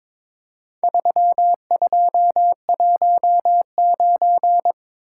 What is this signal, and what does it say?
Morse code, heard as a keyed tone, about 22 wpm, 700 Hz. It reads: 3219
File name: practice / 22wpm / Four Letter-Number Combinations - 22wpm